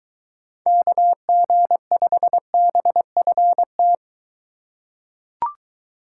Morse code, heard as a keyed tone, about 22 wpm, 700 Hz. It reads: KG5BFT